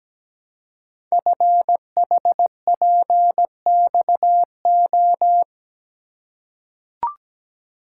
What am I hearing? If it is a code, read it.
Morse code, heard as a keyed tone, about 17 wpm, 700 Hz. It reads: FHPXO
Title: Code Groups - 5 Letters - 17wpm